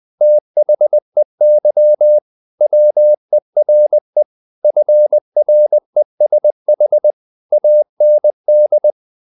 Morse code, heard as a keyed tone, about 20 wpm, 600 Hz. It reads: THEY WERE FRESH AND
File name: the old man and the sea part 2 20wpm